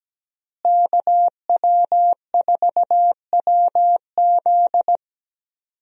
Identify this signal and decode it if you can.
Morse code, heard as a keyed tone, about 17 wpm, 700 Hz. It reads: KW4WZ
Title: USA Call Signs - 5 Characters - Repeated 2x - 17wpm